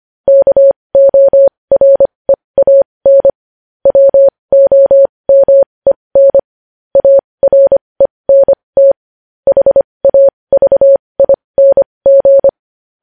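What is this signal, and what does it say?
Morse code, heard as a keyed tone, about 25 wpm, spaced as for 20 wpm, 550 Hz. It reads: KOREAN WOMEN ARENT HAVING
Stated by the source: Morse code news headlines practice at 20 wpm.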